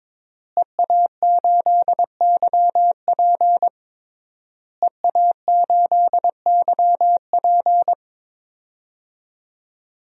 Morse code, heard as a keyed tone, about 21 wpm, 700 Hz. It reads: EA8YP EA8YP
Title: DXCC-100 Call Signs - 5 Characters - Repeated 3x - 22wpm